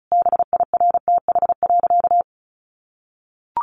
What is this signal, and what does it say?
Morse code, heard as a keyed tone, about 34 wpm, 700 Hz. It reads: BIRTH.
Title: Sentences from Top 900 Words 35wpm